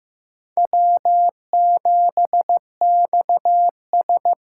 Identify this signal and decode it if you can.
Morse code, heard as a keyed tone, about 15 wpm, 700 Hz. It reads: W7XS
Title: USA Call Signs - 4 Characters - 4x Character Spacing - 15wpm